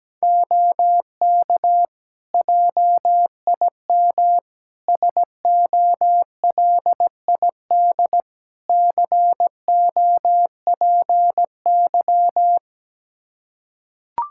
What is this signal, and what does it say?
Morse code, heard as a keyed tone, about 17 wpm, 700 Hz. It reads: OK JIM SOLID COPY
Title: Practice QSO - Exchange 3 - More Info - 17wpm